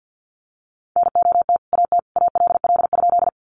Morse code, heard as a keyed tone, about 50 wpm, 700 Hz. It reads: NOT AT ALL?
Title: Sentences from Top 600 Words 50wpm